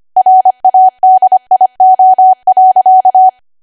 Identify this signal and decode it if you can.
Morse code, heard as a keyed tone, about 25 wpm, 750 Hz. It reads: RADIO.
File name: Archive / 25 WPM / 25WPM